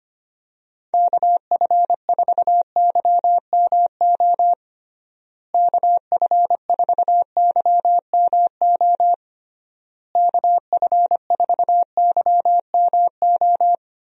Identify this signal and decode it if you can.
Morse code, heard as a keyed tone, about 25 wpm, 700 Hz. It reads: KF4YMO KF4YMO KF4YMO
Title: USA Call Signs - 6 Characters - Repeated 3x - 25wpm